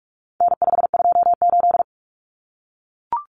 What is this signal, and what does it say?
Morse code, heard as a keyed tone, about 44 wpm, 700 Hz. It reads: N528
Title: Common Cut Numbers - 4-Digit 45wpm